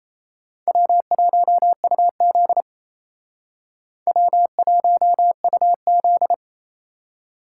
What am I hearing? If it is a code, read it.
Morse code, heard as a keyed tone, about 35 wpm, 700 Hz. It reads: W1UZ W1UZ